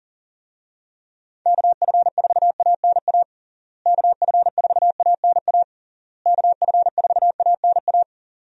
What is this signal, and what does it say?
Morse code, heard as a keyed tone, about 37 wpm, 700 Hz. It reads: KF4ANA KF4ANA KF4ANA
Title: USA Call Signs - 6 Characters - Repeated 3x - 40wpm